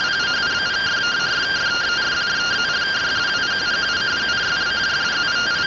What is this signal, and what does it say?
Two-tone teleprinter signal, about 170 Hz shift.
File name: findings / CT AXIAL -377.gen-10 rtty rx